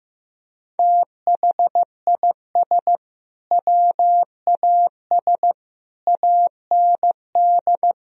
Morse code, heard as a keyed tone, about 15 wpm, 700 Hz. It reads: THIS WAS AND
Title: Sentences from Top 1000 Words - Review Part 4 - 15wpm